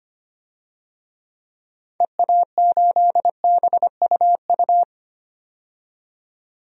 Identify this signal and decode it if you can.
Morse code, heard as a keyed tone, about 25 wpm, 700 Hz. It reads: EA8BUU
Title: DXCC-100 Call Signs - 6 Characters - 25wpm